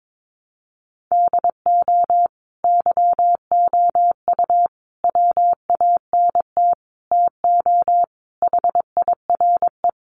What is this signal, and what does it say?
Morse code, heard as a keyed tone, about 22 wpm, 700 Hz. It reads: DO YOU WANT TO HIRE
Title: Sentences from Top 1400 Words 22wpm